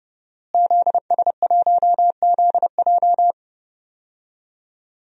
Morse code, heard as a keyed tone, about 30 wpm, 700 Hz. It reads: ZS1ZJ
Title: DXCC-100 Call Signs - 5 Characters - Repeated 3x - 30wpm